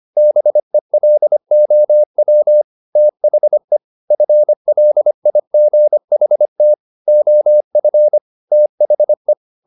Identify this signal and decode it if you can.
Morse code, heard as a keyed tone, about 25 wpm, 600 Hz. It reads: BELOW THE FLIGHT OF THE
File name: the old man and the sea part 1 25wpm